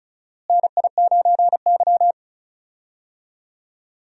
Morse code, heard as a keyed tone, about 34 wpm, 700 Hz. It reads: NI9Y
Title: USA Call Signs - 4 Characters - Repeated 3x - 35wpm